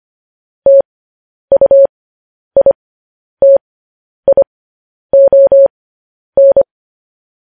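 Morse code, heard as a keyed tone, about 25 wpm, spaced as for 10 wpm, 550 Hz. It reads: TUITION
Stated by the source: Morse news headlines sent at 10 wpm.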